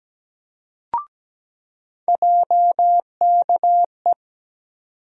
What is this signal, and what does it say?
Morse code, heard as a keyed tone, about 17 wpm, 700 Hz. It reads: JKE
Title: USA Call Sign Postfixes - 3 Letters 17wpm